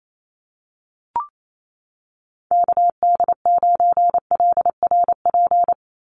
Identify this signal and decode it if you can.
Morse code, heard as a keyed tone, about 28 wpm, 700 Hz. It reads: KD9LRP
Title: USA Call Signs 28wpm